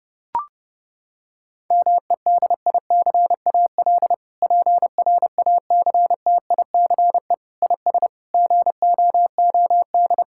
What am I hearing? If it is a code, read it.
Morse code, heard as a keyed tone, about 30 wpm, 700 Hz. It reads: MEDICAL PRACTICE IS GOOD
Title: Sentences from Top 1000 Words - Review Part 5 - 30wpm